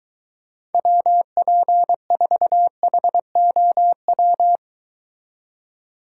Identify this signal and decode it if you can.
Morse code, heard as a keyed tone, about 22 wpm, 700 Hz. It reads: WP4HOW